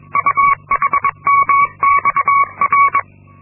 Morse code, heard as a keyed tone, about 22 wpm, 1120 Hz. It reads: UHMXR
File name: Linksmakalnis-2